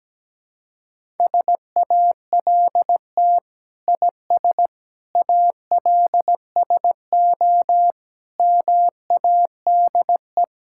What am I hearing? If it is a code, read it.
Morse code, heard as a keyed tone, about 17 wpm, 700 Hz. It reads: SALT IS ALSO MADE
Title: Sentences from Top 1000 Words - Review Part 5 - 17wpm